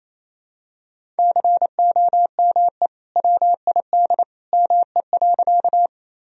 Morse code, heard as a keyed tone, about 28 wpm, 700 Hz. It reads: COME WID ME.
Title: Sentences from Top 200 Words and Abbreviations 28wpm